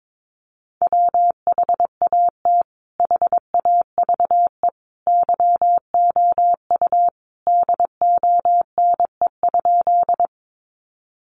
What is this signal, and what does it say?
Morse code, heard as a keyed tone, about 22 wpm, 700 Hz. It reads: WHAT HAVE YOU DONE?
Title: Sentences from Top 500 Words 22wpm